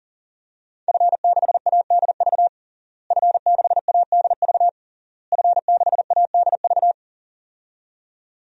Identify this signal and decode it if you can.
Morse code, heard as a keyed tone, about 40 wpm, 700 Hz. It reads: F6ADV F6ADV F6ADV
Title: DXCC-100 Call Signs - 5 Characters - Repeated 3x - 40wpm